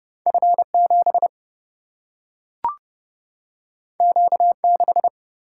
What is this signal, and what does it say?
Morse code, heard as a keyed tone, about 30 wpm, 700 Hz. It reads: F 7 Q 6
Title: Single-Letter Single-Number 30wpm